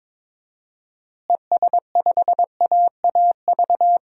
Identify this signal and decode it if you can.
Morse code, heard as a keyed tone, about 22 wpm, 700 Hz. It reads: ES5AAV
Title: DXCC-100 Call Signs - 6 Characters - Repeated 3x - 22wpm